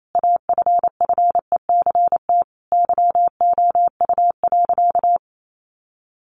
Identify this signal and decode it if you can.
Morse code, heard as a keyed tone, about 28 wpm, 700 Hz. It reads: AFFECT YOU.
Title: Sentences from Top 900 Words 28wpm